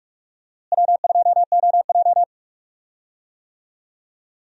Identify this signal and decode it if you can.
Morse code, heard as a keyed tone, about 44 wpm, 700 Hz. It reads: W2OJ
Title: USA Call Signs - 4 Characters - Repeated 3x - 45wpm